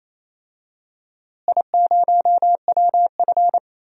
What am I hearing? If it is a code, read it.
Morse code, heard as a keyed tone, about 28 wpm, 700 Hz. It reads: I0WF
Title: DXCC-100 Call Signs - 4 Characters - Repeated 2x - 28wpm